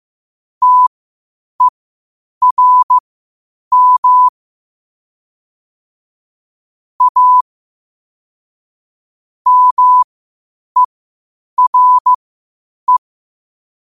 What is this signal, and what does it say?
Morse code, heard as a keyed tone, about 15 wpm, spaced as for 8 wpm, 1000 Hz. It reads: TERM A MERE
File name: qotdfiles / 8wpm